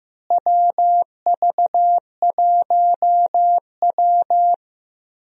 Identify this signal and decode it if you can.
Morse code, heard as a keyed tone, about 15 wpm, 700 Hz. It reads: WV1W
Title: USA Call Signs - 4 Characters - Repeated 3x - 15wpm